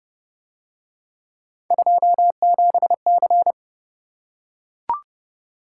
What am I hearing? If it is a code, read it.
Morse code, heard as a keyed tone, about 29 wpm, 700 Hz. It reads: 27C